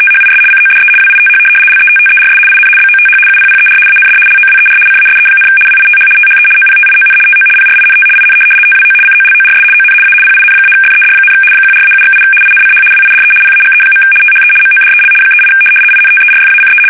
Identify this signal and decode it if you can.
RTTY, about 75 baud. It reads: THIS CODE WITH LETTERS IN BOTH UPPER AND LOWER CASE, WITH THE OPTION TO PRINT IN TWO COLORS ON A PIN-FED, 14-7/8 INCH WIDE PAGE. THE MODEL 38 INTERFACE
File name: fsk850-75B / teletype-corp